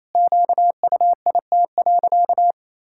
Morse code, heard as a keyed tone, about 28 wpm, 700 Hz. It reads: QUIT.
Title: Pangram Challenge v2 - 28wpm